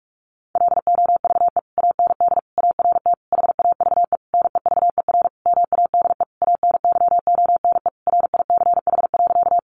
Morse code, heard as a keyed tone, about 44 wpm, 700 Hz. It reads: LOVE AND ART HAVE NEVER MADE ANYONE RICH.
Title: Sentences from Top 1100 Words 45wpm